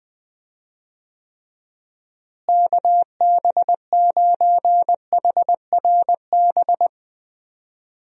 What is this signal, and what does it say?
Morse code, heard as a keyed tone, about 20 wpm, 700 Hz. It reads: KB9HRB
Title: USA Call Signs - 6 Characters - Repeated 2x - 20wpm